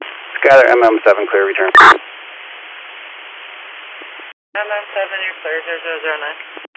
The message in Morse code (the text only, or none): none